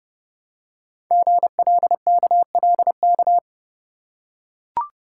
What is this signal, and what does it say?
Morse code, heard as a keyed tone, about 30 wpm, 700 Hz. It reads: GLKLK
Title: Code Groups - 5 Letters - 30wpm